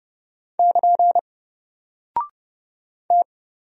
Morse code, heard as a keyed tone, about 30 wpm, 700 Hz. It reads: <KN> T
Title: Single Character (Letter:Number:Punctuation:Prosign) 30wpm